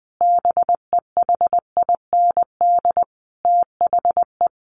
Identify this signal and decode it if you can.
Morse code, heard as a keyed tone, about 20 wpm, 700 Hz. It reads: BEHIND THE
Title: A Sound Of Thunder 20wpm